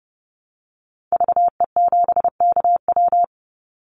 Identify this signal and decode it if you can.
Morse code, heard as a keyed tone, about 30 wpm, 700 Hz. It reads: VE7KW
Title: Call Signs - 30wpm